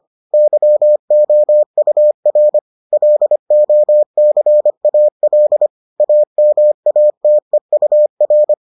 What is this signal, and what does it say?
Morse code, heard as a keyed tone, about 25 wpm, 600 Hz. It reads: YOUR LOCAL AMATEUR